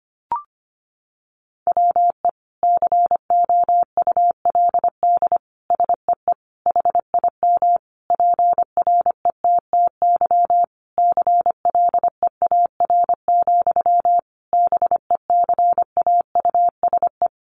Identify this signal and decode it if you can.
Morse code, heard as a keyed tone, about 25 wpm, 700 Hz. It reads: WE COULD SEE HIM PRETTY CLEAR, BECAUSE
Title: Sentences from Top 1000 Words 25wpm